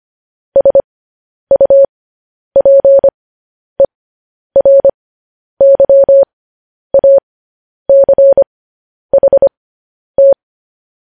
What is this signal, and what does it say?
Morse code, heard as a keyed tone, about 25 wpm, spaced as for 10 wpm, 550 Hz. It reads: SUPERYACHT
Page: news-headlines-in-morse-code-at-10-wpm